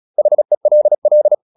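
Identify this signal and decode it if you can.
Morse code, heard as a keyed tone, about 36 wpm, 600 Hz. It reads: HELL
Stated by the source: Morse practice at 35 wpm from CW med Gnister, sending Ken Follett's novel Never in English.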